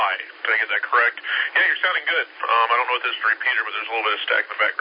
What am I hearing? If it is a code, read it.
Emergency Alert System (SAME) tones, heard as harsh buzzing bursts, with speech. It is heard from 0.3 to 3.7 s.